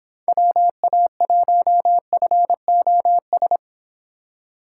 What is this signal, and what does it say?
Morse code, heard as a keyed tone, about 30 wpm, 700 Hz. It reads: WA1FOS